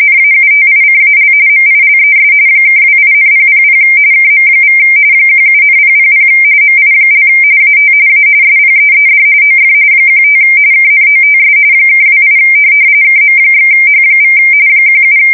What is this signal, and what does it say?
RTTY, about 45.45 baud. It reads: NAUTICAL MILES (28,000 KM= 17,000 MI) AT A SPEED OF 15 KNOTS (28 KM/H= 17